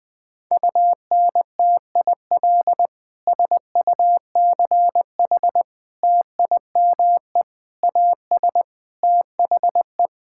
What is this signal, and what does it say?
Morse code, heard as a keyed tone, about 20 wpm, 700 Hz. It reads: UNTIL SUCH TIME AS THE
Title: Sets of 9 Words - 20wpm